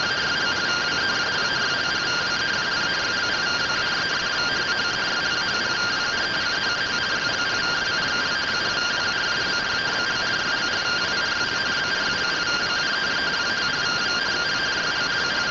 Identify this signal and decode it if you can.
Two-tone teleprinter signal, about 170 Hz shift.